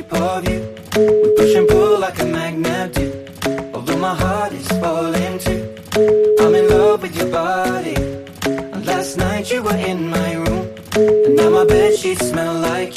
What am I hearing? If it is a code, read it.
Ringback tone (Europe, 425 Hz), with noise.